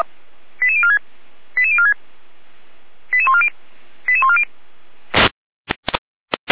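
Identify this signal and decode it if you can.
Five-tone selective call (ZVEI), heard as quick runs of five tones, tones of about 70 ms. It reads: 88046 88046 88140 88140
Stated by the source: ZVEI five-tone call received by SDR.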